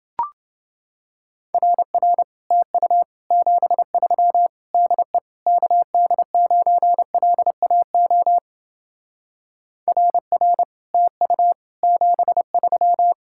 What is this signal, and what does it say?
Morse code, heard as a keyed tone, about 28 wpm, 700 Hz. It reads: RR TU 73 DE KD9LAO RR TU 73